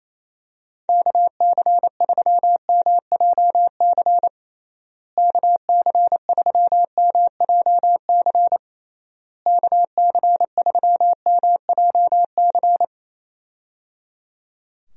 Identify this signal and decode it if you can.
Morse code, heard as a keyed tone, about 28 wpm, 700 Hz. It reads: KC3MJC KC3MJC KC3MJC
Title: USA Call Signs - 6 Characters - Repeated 3x - 28wpm